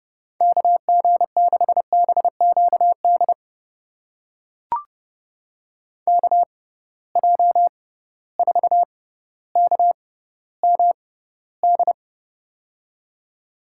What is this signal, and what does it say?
Morse code, heard as a keyed tone, about 31 wpm, spaced as for 16 wpm, 700 Hz. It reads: KG6BQD KJ4KMD
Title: USA Call Signs - 6 Characters - 6x Character Spacing - 30wpm